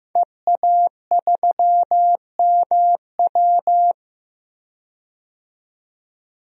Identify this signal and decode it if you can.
Morse code, heard as a keyed tone, about 15 wpm, 700 Hz. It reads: EA3MW
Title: DXCC-100 Call Signs - 5 Characters - 15wpm